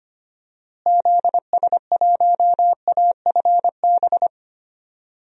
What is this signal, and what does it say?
Morse code, heard as a keyed tone, about 25 wpm, 700 Hz. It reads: ZS1AFB
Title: DXCC-100 Call Signs - 6 Characters - Repeated 3x - 25wpm